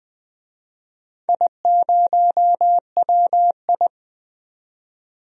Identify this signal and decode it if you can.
Morse code, heard as a keyed tone, about 20 wpm, 700 Hz. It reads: I0WI
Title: DXCC-100 Call Signs - 4 Characters - 20wpm